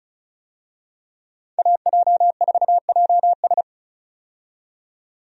Morse code, heard as a keyed tone, about 36 wpm, 700 Hz. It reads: AJ4JS
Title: USA Call Signs - 5 Characters - 35wpm